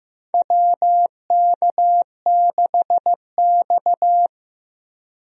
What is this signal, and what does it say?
Morse code, heard as a keyed tone, about 15 wpm, 700 Hz. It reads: WK6X
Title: USA Call Signs - 4 Characters - Repeated 2x - 15wpm